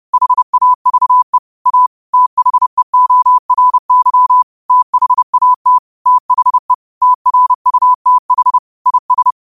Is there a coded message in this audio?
Yes: Morse code from 0.1 to 9.3 s, heard as a keyed tone, about 30 wpm, 1000 Hz. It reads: HAVE A THEORY THAT THE TRUTH IS